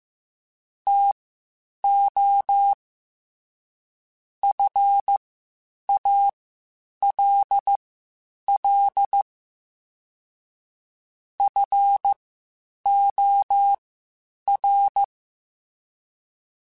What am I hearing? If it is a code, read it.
Morse code, heard as a keyed tone, about 15 wpm, spaced as for 8 wpm, 800 Hz. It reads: TO FALL FOR
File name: Quotes Spaced / Quotes Spaced-mp3s / StandFor sp @13